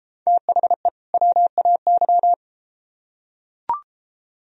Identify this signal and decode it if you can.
Morse code, heard as a keyed tone, about 31 wpm, 700 Hz. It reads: THE WAY